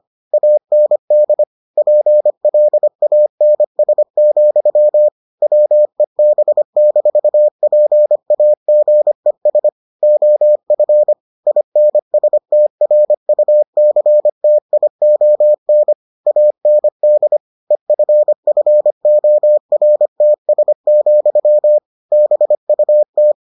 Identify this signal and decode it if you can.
Morse code, heard as a keyed tone, about 25 wpm, 600 Hz. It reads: AND PLANS, WEB-PAGES OF INSTRUCTION AND EFFORTS, BUT